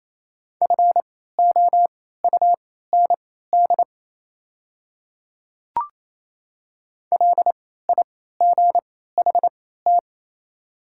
Morse code, heard as a keyed tone, about 27 wpm, spaced as for 16 wpm, 690 Hz. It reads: FOUND LIGHT
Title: 3x Character Spacing - Top 500 Words - 28wpm